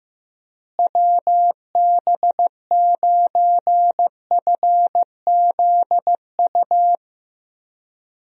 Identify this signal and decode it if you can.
Morse code, heard as a keyed tone, about 16 wpm, 700 Hz. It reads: WB9FZU
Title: USA Call Signs - 6 Characters - Repeated 2x - 15wpm